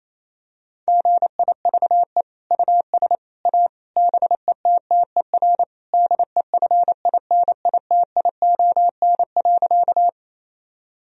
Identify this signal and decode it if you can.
Morse code, heard as a keyed tone, about 28 wpm, 700 Hz. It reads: GIVE US A BETTER DEFINITION.
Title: Sentences from Top 1000 Words - Review Part 4 - 28wpm